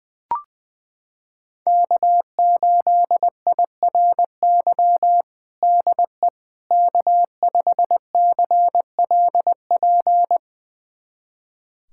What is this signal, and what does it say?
Morse code, heard as a keyed tone, about 21 wpm, 700 Hz. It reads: K8IRY DE K5CLP
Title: USA Callsign DE Callsign - 5 Characters - 20wpm